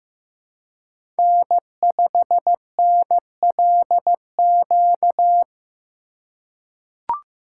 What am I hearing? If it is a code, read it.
Morse code, heard as a keyed tone, about 15 wpm, 700 Hz. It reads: N5NLQ
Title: USA Call Signs - 5 Characters - 2x Character Spacing - 15wpm